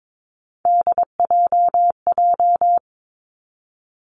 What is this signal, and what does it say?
Morse code, heard as a keyed tone, about 22 wpm, 700 Hz. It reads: DJJ
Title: Three Challenging Letter-Number Combinations 22wpm